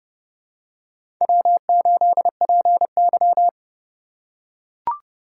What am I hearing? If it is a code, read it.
Morse code, heard as a keyed tone, about 30 wpm, 700 Hz. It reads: W8PY